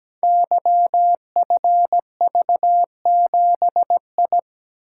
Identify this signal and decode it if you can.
Morse code, heard as a keyed tone, about 17 wpm, 700 Hz. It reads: YFV7I
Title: Code Groups - 5 Letter:Numbers - 17wpm